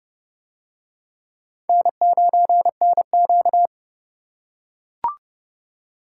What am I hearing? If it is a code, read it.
Morse code, heard as a keyed tone, about 29 wpm, 700 Hz. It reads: N9NQ